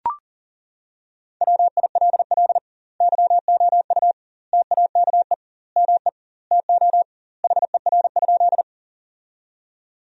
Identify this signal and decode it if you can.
Morse code, heard as a keyed tone, about 40 wpm, 700 Hz. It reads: WILL YOU TAKE ME TO HER?
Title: 2x Word-Spacing - Sentences from Top 200 Words - 40wpm